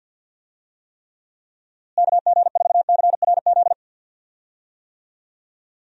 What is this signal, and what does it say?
Morse code, heard as a keyed tone, about 46 wpm, 700 Hz. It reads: KG4CRZ
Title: USA Call Signs - 6 Characters - Repeated 2x - 50wpm